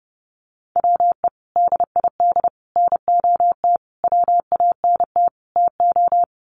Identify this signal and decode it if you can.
Morse code, heard as a keyed tone, about 30 wpm, 700 Hz. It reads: WE DID NOT WANT TO
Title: Sets of 5 Words Encore 30wpm